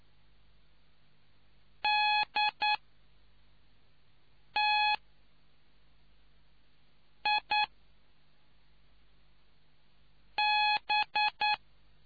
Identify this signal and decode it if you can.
Morse code, heard as a keyed tone, about 9 wpm, 840 Hz. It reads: D T I B